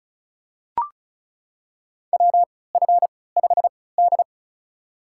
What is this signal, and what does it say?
Morse code, heard as a keyed tone, about 36 wpm, spaced as for 20 wpm, 700 Hz. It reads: WF5D
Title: USA Call Signs - 4 Characters - 3x Character Spacing - 35wpm